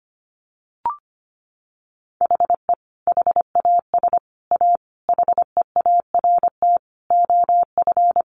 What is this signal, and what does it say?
Morse code, heard as a keyed tone, about 25 wpm, 700 Hz. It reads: HE HAS A HEART OF
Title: Sentences from Top 1200 Words 25wpm